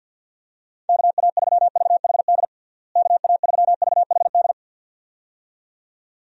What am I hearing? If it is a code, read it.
Morse code, heard as a keyed tone, about 52 wpm, 700 Hz. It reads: KA3VHD KA3VHD
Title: USA Call Signs - 6 Characters - Repeated 2x - 50wpm